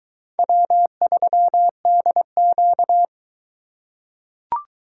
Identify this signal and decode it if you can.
Morse code, heard as a keyed tone, about 22 wpm, 700 Hz. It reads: W3DQ